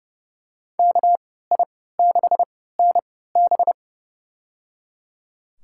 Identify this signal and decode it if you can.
Morse code, heard as a keyed tone, about 30 wpm, spaced as for 17 wpm, 700 Hz. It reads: KI6NB
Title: USA Call Signs - 3x Character Spacing - 30wpm